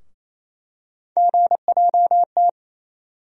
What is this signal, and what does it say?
Morse code, heard as a keyed tone, about 28 wpm, 700 Hz. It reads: GJT
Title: USA Call Sign Postfixes - 3 Letters 28wpm